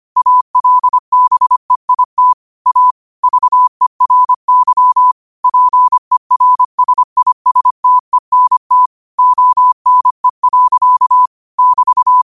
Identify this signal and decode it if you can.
Morse code, heard as a keyed tone, about 25 wpm, 1000 Hz. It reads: ALBEIT A VERY PERSISTENT ONE. <BT>
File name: qotdfiles / 25wpm